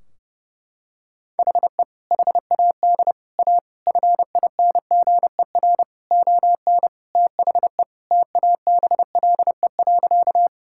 Morse code, heard as a keyed tone, about 30 wpm, 700 Hz. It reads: HE HAD A FINGER ON THE TABLE.
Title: Sentences from Top 1600 Words 30wpm